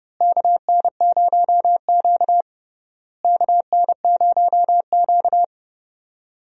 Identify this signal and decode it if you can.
Morse code, heard as a keyed tone, about 30 wpm, 700 Hz. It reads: KN0Q KN0Q
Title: USA Call Signs - 4 Characters - Repeated 3x - 30wpm